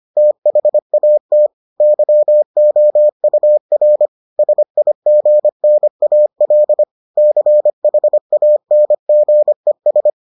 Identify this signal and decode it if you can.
Morse code, heard as a keyed tone, about 25 wpm, 600 Hz. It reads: THAT YOUR SIGNAL CHANGES